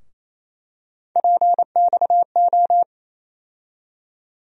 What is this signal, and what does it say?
Morse code, heard as a keyed tone, about 28 wpm, 700 Hz. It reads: PXO
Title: USA Call Sign Postfixes - 3 Letters 28wpm